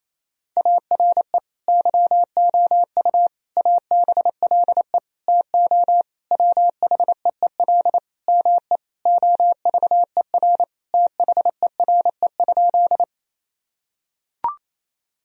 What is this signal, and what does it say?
Morse code, heard as a keyed tone, about 28 wpm, 700 Hz. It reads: ARE YOU ABLE TO WHEEL ME OVER THERE?
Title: Sentences from Top 1000 Words - Review Part 3 - 28wpm